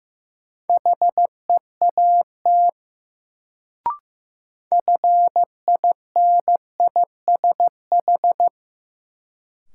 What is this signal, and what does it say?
Morse code, heard as a keyed tone, about 15 wpm, 700 Hz. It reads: HEAT FINISH
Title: Sentences from Top 1000 Words - Review Part 3 - New Word Drill - 15wpm